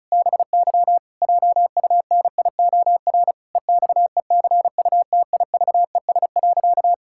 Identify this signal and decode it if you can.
Morse code, heard as a keyed tone, about 34 wpm, 700 Hz. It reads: BY JUNIOR EXECUTIVES.
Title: Pangram Challenge v2 - 35wpm